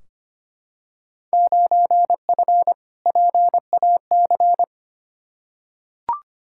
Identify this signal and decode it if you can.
Morse code, heard as a keyed tone, about 25 wpm, 700 Hz. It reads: 9F PAC
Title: Field Day Exchanges Encore - 25wpm